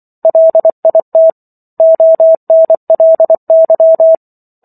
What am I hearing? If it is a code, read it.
Morse code, heard as a keyed tone, about 24 wpm, 650 Hz. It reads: LIT ONLY